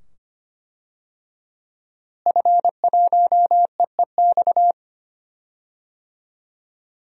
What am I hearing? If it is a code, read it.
Morse code, heard as a keyed tone, about 25 wpm, 700 Hz. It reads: F1EEX
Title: DXCC-100 Call Signs - 5 Characters - Repeated 2x - 25wpm